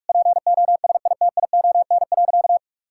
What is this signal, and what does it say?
Morse code, heard as a keyed tone, about 41 wpm, 700 Hz. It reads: POSITION.
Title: Sentences from Top 1000 Words - Review Part 5 - 45wpm